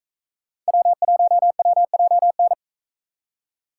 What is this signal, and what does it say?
Morse code, heard as a keyed tone, about 40 wpm, 700 Hz. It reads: W1WJN